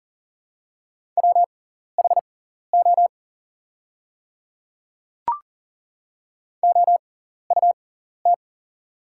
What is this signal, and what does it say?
Morse code, heard as a keyed tone, about 40 wpm, spaced as for 14 wpm, 700 Hz. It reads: WHO OUT
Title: 6x Character Spacing - Top 300 Words - 40wpm